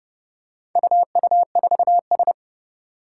Morse code, heard as a keyed tone, about 30 wpm, 700 Hz. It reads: UU4S
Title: Four Challenging Letter-Number Combinations 30wpm